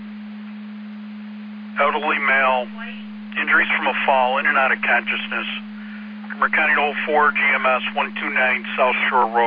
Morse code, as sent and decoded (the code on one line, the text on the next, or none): none